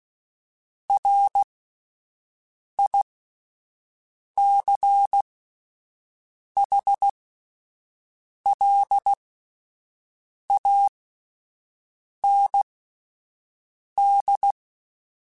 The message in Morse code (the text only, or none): RICHLAND